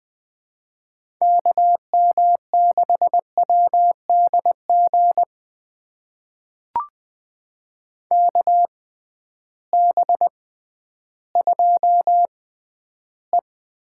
Morse code, heard as a keyed tone, about 20 wpm, spaced as for 12 wpm, 700 Hz. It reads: KM6WDG KB2E
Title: USA Call Signs - 6 Characters - 6x Character Spacing - 20wpm